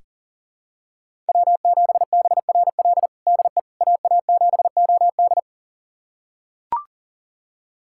Morse code, heard as a keyed tone, about 40 wpm, 700 Hz. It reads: W7BRL DE AA7OD
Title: USA Callsign DE Callsign - 5 Characters - 40wpm